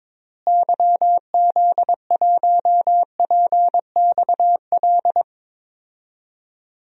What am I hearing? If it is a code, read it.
Morse code, heard as a keyed tone, about 23 wpm, 700 Hz. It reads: YZ1PXL
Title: DXCC-100 Call Signs - 6 Characters - Repeated 3x - 22wpm